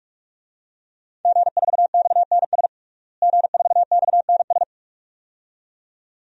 Morse code, heard as a keyed tone, about 43 wpm, 700 Hz. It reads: G4XNS G4XNS
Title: DXCC-100 Call Signs - 5 Characters - Repeated 2x - 45wpm